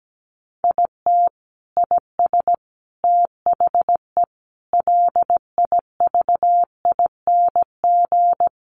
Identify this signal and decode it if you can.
Morse code, heard as a keyed tone, about 17 wpm, 700 Hz. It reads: IT IS THE LIVING
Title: Sentences from Top 1100 Words 17wpm